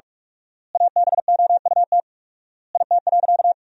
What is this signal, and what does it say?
Morse code, heard as a keyed tone, about 41 wpm, 700 Hz. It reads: ABOUT IT.
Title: 4x Word-Spacing - Sentences from Top 500 Words - 45wpm